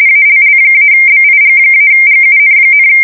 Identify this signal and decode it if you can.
RTTY, about 45.45 baud. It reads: ALL-BIG-GUN,
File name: fsk170 / USS-New-Jersey